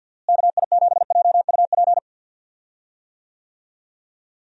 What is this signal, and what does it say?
Morse code, heard as a keyed tone, about 49 wpm, 700 Hz. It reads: KI7JUP
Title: USA Call Signs - 6 Characters - Repeated 2x - 50wpm